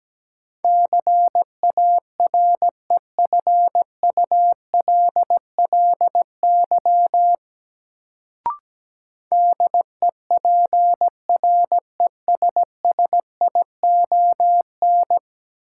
Morse code, heard as a keyed tone, about 17 wpm, 700 Hz. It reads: CAREFULLY DEPRESSION
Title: Sentences from Top 1300 Words - New Word Drill - 17wpm